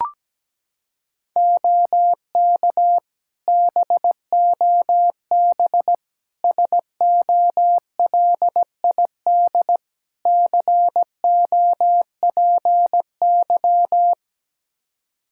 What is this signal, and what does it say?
Morse code, heard as a keyed tone, about 17 wpm, 700 Hz. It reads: OK BOB SOLID COPY
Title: Practice QSO - Exchange 3 - More Info - 17wpm